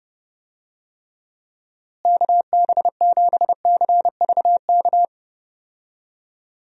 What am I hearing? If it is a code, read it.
Morse code, heard as a keyed tone, about 30 wpm, 700 Hz. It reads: KB7CVK
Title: USA Call Signs - 6 Characters - Repeated 2x - 30wpm